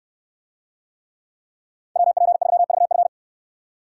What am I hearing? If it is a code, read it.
Morse code, heard as a keyed tone, about 63 wpm, 700 Hz. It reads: WP3VF